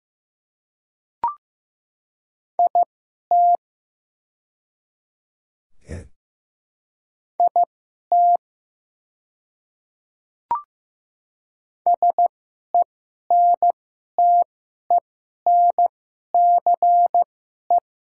Morse code, heard as a keyed tone, about 15 wpm, spaced as for 11 wpm, 700 Hz. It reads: IT IT SENTENCE
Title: 2x Character Spacing - Top 300 Words - 15wpm